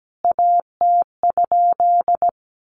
Morse code, heard as a keyed tone, about 17 wpm, 700 Hz. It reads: AT?
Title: A Sound Of Thunder 17wpm